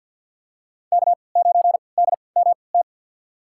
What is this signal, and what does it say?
Morse code, heard as a keyed tone, about 53 wpm, spaced as for 29 wpm, 700 Hz. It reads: K9DMT
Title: USA Call Signs - 5 Characters - 3x Character Spacing - 50wpm